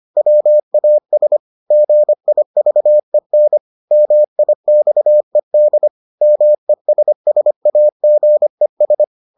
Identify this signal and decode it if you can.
Morse code, heard as a keyed tone, about 25 wpm, 600 Hz. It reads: WAS GIVEN MIXED MESSAGES